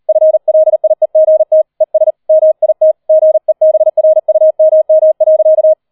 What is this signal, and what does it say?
Morse code, heard as a keyed tone, about 38 wpm, 630 Hz. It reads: FLIEGT ES MIT GEBRUMM.
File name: 40wpm